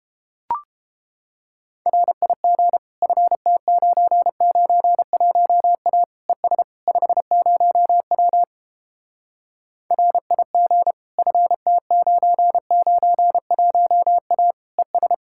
Morse code, heard as a keyed tone, about 29 wpm, 700 Hz. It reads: RIG FT991A ES 50W RIG FT991A ES